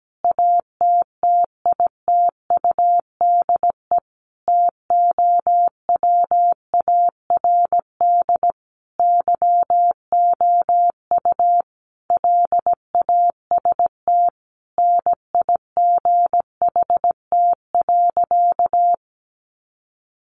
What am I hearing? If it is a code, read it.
Morse code, heard as a keyed tone, about 17 wpm, 700 Hz. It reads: ATTITUDE TOWARD YOU LAST NIGHT.
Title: Sentences from Top 1300 Words 17wpm